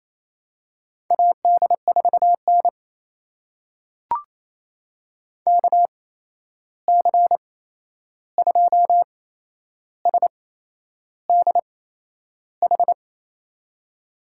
Morse code, heard as a keyed tone, about 28 wpm, spaced as for 11 wpm, 700 Hz. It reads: AD4N KC2SDH
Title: USA Call Signs - 8x Character Spacing - 28wpm